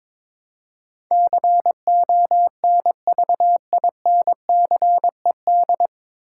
Morse code, heard as a keyed tone, about 22 wpm, 700 Hz. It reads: CONVINCED
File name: 9 Letter Words - First 3 Letters Given - 22wpm